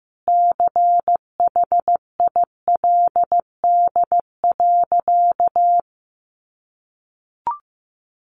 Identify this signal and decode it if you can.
Morse code, heard as a keyed tone, about 15 wpm, 700 Hz. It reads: CHILD.
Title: Sentences from Top 1100 Words 15wpm